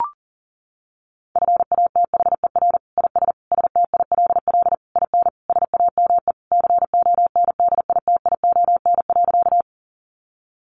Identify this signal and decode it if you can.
Morse code, heard as a keyed tone, about 40 wpm, 700 Hz. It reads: FATHER IS STILL IN SAME CONDITION.
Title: Sentences from Top 1000 Words 40wpm